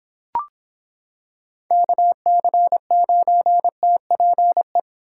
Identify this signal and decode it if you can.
Morse code, heard as a keyed tone, about 26 wpm, 700 Hz. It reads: KC9TPE